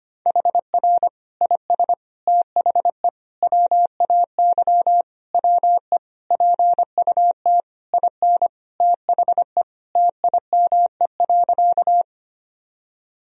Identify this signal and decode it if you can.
Morse code, heard as a keyed tone, about 25 wpm, 700 Hz. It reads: HR IS THE WAY WE PUT IN THE TIME.
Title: Sentences from Top 200 Words and Abbreviations 25wpm